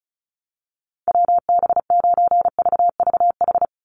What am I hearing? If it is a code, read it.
Morse code, heard as a keyed tone, about 34 wpm, 700 Hz. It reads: WB9VVH
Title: USA Call Signs 35wpm